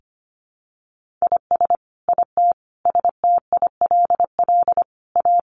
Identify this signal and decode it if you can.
Morse code, heard as a keyed tone, about 25 wpm, 700 Hz. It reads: IS IT STILL A
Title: Sentences from Top 1500 Words 25wpm